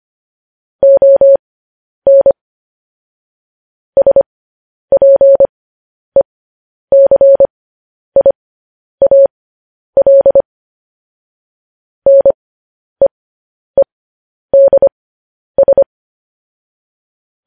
Morse code, heard as a keyed tone, about 25 wpm, spaced as for 10 wpm, 550 Hz. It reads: ON SPECIAL NEEDS
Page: news headlines in morse code at 10 wpm